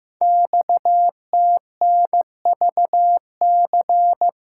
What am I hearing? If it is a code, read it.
Morse code, heard as a keyed tone, about 15 wpm, 700 Hz. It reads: XTNVC
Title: Code Groups - 5 Letter:Numbers - 15wpm